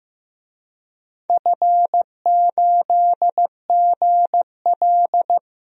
Morse code, heard as a keyed tone, about 15 wpm, 700 Hz. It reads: F8GL
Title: DXCC-100 Call Signs - 4 Characters - 15wpm